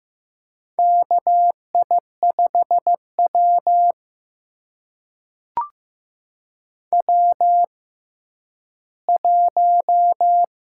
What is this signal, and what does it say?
Morse code, heard as a keyed tone, about 15 wpm, spaced as for 10 wpm, 700 Hz. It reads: KI5W W1
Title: USA Call Signs - 6x Character Spacing - 15wpm